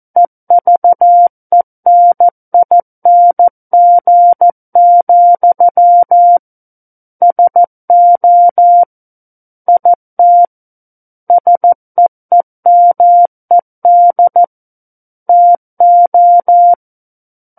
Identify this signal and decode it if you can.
Morse code, heard as a keyed tone, about 14 wpm, 700 Hz. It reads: EVENING, SO IT SEEMED TO
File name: Book One / 05-Chapter 5 - 14 WPM